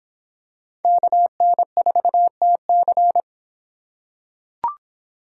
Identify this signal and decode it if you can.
Morse code, heard as a keyed tone, about 24 wpm, 700 Hz. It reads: KN4TC